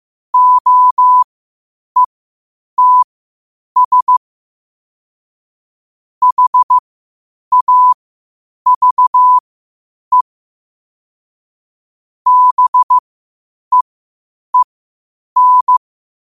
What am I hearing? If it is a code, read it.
Morse code, heard as a keyed tone, about 15 wpm, spaced as for 8 wpm, 1000 Hz. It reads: OETS HAVE BEEN